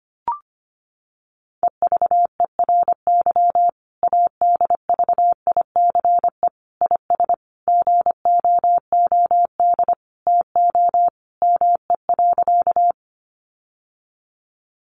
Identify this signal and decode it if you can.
Morse code, heard as a keyed tone, about 25 wpm, 700 Hz. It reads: EVERY ADVICE IS GOOD TO ME.
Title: Sentences from Top 1100 Words 25wpm